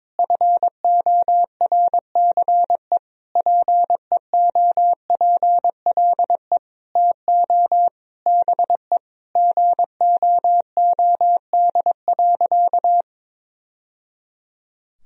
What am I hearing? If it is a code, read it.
Morse code, heard as a keyed tone, about 22 wpm, 700 Hz. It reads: FORCE PEOPLE TO BE GOOD.
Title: Sentences from Top 1000 Words - Review Part 3 - 22wpm